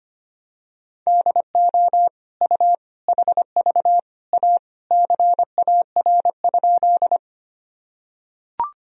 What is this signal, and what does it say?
Morse code, heard as a keyed tone, about 25 wpm, 700 Hz. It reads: DO U HV A CAR?
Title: Sentences from Top 300 Words and Abbreviations 25wpm